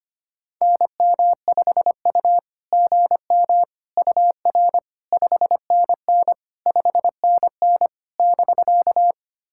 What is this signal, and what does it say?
Morse code, heard as a keyed tone, about 25 wpm, 690 Hz. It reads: NM5U GM UR 5NN 5NN <BK>
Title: SOTA QSO Practice 25wpm